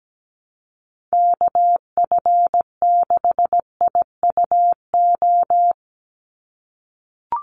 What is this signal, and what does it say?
Morse code, heard as a keyed tone, about 17 wpm, 700 Hz. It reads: KF6IUO
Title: USA Call Signs 17wpm